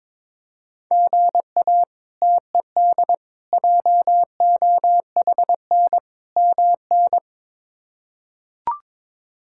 Morse code, heard as a keyed tone, about 22 wpm, 700 Hz. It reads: GA TED JOHN MN
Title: K1USN SST QSO Practice - 22wpm